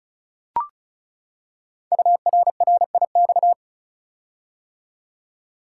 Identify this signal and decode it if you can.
Morse code, heard as a keyed tone, about 36 wpm, 700 Hz. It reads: URRIX
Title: Code Groups - 5 Letters - 35wpm